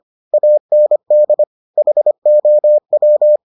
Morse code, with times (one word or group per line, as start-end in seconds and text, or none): 0.3-1.4 AND
1.8-3.4 HOW